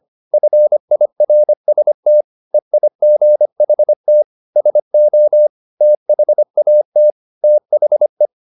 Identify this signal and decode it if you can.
Morse code, heard as a keyed tone, about 25 wpm, 600 Hz. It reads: FIRST EIGHT SO THAT THE